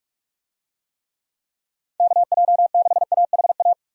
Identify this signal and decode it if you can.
Morse code, heard as a keyed tone, about 44 wpm, 700 Hz. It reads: KJ6AHA
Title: USA Call Signs - 6 Characters - 45wpm